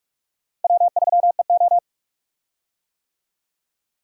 Morse code, heard as a keyed tone, about 46 wpm, 700 Hz. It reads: W3EO
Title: USA Call Signs - 4 Characters - Repeated 3x - 45wpm